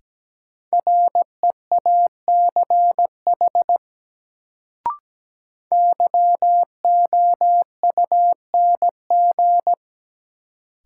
Morse code, heard as a keyed tone, about 17 wpm, 700 Hz. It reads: REACH YOUNG
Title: Sentences from Top 400 Words - New Word Drill - 17wpm